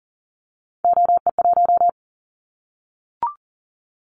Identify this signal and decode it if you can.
Morse code, heard as a keyed tone, about 40 wpm, 700 Hz. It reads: OE1
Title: Callsign Prefixes for Top 100 DXCC - 40wpm